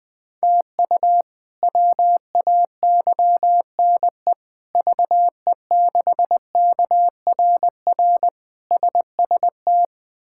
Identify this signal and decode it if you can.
Morse code, heard as a keyed tone, about 20 wpm, 700 Hz. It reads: TU WAYNE VE6KRR SST
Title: K1USN SST QSO Practice - 20wpm